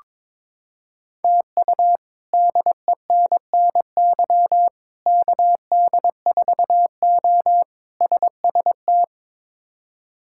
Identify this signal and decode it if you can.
Morse code, heard as a keyed tone, about 22 wpm, 700 Hz. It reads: TU DENNY KD4O SST
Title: K1USN SST QSO Practice - 22wpm